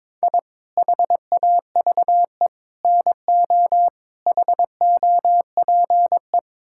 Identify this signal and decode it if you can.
Morse code, heard as a keyed tone, about 21 wpm, 700 Hz. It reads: I HAVE NO HOPE
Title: Sentences from Top 1000 Words - Review Part 5 - 22wpm